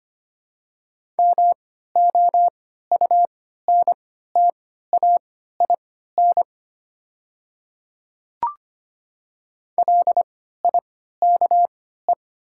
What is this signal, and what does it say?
Morse code, heard as a keyed tone, about 25 wpm, spaced as for 14 wpm, 700 Hz. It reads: MOUNTAIN LIKE
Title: 3x Character Spacing - Top 300 Words - 25wpm